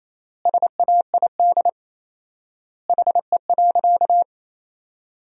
Morse code, heard as a keyed tone, about 28 wpm, 700 Hz. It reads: SAID HE.
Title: 4x Word-Spacing - Sentences from Top 400 Words - 28wpm